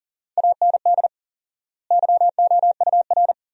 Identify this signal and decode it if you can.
Morse code, heard as a keyed tone, about 40 wpm, 700 Hz. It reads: AND YOUR
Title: 4x Word-Spacing - Sentences from Top 300 Words - 40wpm